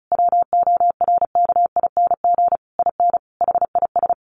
Morse code, heard as a keyed tone, about 34 wpm, 700 Hz. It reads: WORKING IN HIS
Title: Sentences from Top 1200 Words 35wpm